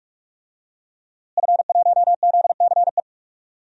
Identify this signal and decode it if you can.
Morse code, heard as a keyed tone, about 45 wpm, 700 Hz. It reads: F1ZCE
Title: DXCC-100 Call Signs - 5 Characters - Repeated 2x - 45wpm